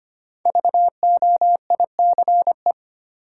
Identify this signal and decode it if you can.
Morse code, heard as a keyed tone, about 25 wpm, 700 Hz. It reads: VOICE
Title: Sentences from Top 500 Words - New Word Drill - 25wpm